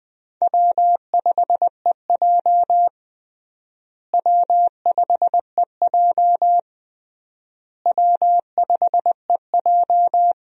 Morse code, heard as a keyed tone, about 21 wpm, 700 Hz. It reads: W5EJ W5EJ W5EJ
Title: USA Call Signs - 4 Characters - Repeated 3x - 20wpm